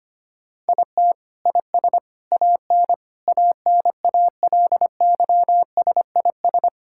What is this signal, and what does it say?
Morse code, heard as a keyed tone, about 25 wpm, 700 Hz. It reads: IT IS AN ANALYSIS
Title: Sentences from Top 1000 Words - Review Part 5 - 25wpm